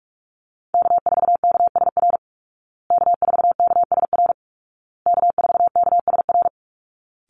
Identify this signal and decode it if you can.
Morse code, heard as a keyed tone, about 44 wpm, 700 Hz. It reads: K4KSR K4KSR K4KSR
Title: Call Signs Repeated 2x - 45wpm